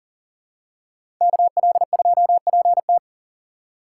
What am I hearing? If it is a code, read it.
Morse code, heard as a keyed tone, about 40 wpm, 700 Hz. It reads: KL2PT
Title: USA Call Signs - 4x Character Spacing - 40wpm